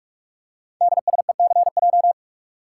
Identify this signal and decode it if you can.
Morse code, heard as a keyed tone, about 44 wpm, 690 Hz. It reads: DSECJ
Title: Code Groups - 5 Letters - 45wpm